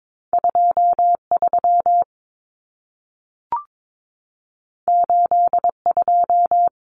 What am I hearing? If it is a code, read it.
Morse code, heard as a keyed tone, about 22 wpm, 700 Hz. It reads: 23 82
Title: Numbers - 2-Digit 22wpm